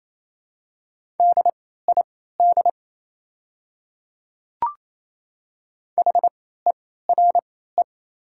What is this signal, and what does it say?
Morse code, heard as a keyed tone, about 28 wpm, spaced as for 16 wpm, 700 Hz. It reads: DID HERE
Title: 3x Character Spacing - Top 300 Words - 28wpm